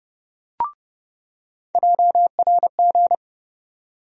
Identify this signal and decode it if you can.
Morse code, heard as a keyed tone, about 30 wpm, 700 Hz. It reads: JRG
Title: USA Call Sign Postfixes - 3 Letters 30wpm